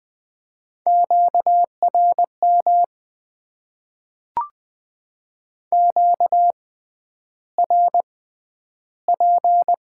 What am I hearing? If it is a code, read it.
Morse code, heard as a keyed tone, about 20 wpm, spaced as for 10 wpm, 700 Hz. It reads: QRM QRP
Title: Top 90 QSO Elements - 6x Character Spacing - 20wpm